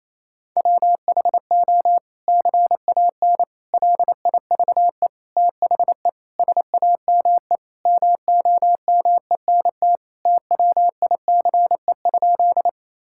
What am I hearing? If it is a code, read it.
Morse code, heard as a keyed tone, about 28 wpm, 700 Hz. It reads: WHO CAN LIVE THE SAME MOMENT TWICE?
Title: Sentences from Top 1600 Words 28wpm